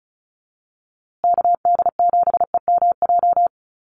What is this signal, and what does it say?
Morse code, heard as a keyed tone, about 34 wpm, 700 Hz. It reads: KD7EMJ
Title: USA Call Signs Repeated 3x 35wpm